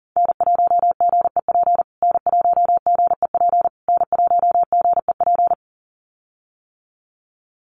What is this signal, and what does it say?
Morse code, heard as a keyed tone, about 37 wpm, 700 Hz. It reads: N1GEP N1GEP N1GEP
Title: USA Call Signs Repeated 3x 40wpm